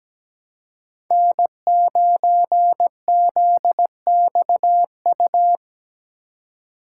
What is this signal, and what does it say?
Morse code, heard as a keyed tone, about 17 wpm, 700 Hz. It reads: N9ZXU
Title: USA Call Signs - 2x Character Spacing - 17wpm